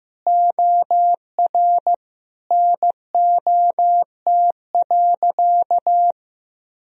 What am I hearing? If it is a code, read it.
Morse code, heard as a keyed tone, about 15 wpm, 700 Hz. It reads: OR NOT.
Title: Sentences from Top 1000 Words - Review Part 4 - 15wpm